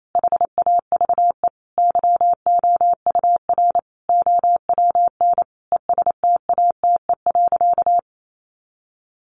Morse code, heard as a keyed tone, about 28 wpm, 700 Hz. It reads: HAVE YOUR OWN ESTATE.
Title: Sentences from Top 1200 Words 28wpm